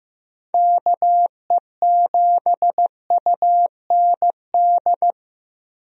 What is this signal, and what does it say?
Morse code, heard as a keyed tone, about 15 wpm, 700 Hz. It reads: KE7UND
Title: USA Call Signs - 6 Characters - Repeated 3x - 15wpm